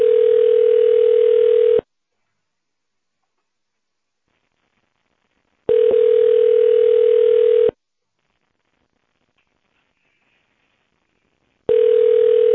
Ringback tone (North America), 440 + 480 Hz.